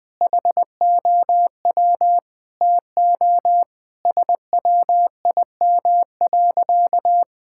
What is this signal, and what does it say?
Morse code, heard as a keyed tone, about 20 wpm, 700 Hz. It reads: HOW TO SWIM.
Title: Sentences with 4-Word Phrases - 20wpm